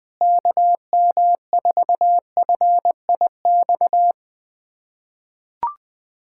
Morse code, heard as a keyed tone, about 20 wpm, 700 Hz. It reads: KM4FIX
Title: USA Callsign DE Callsign 20wpm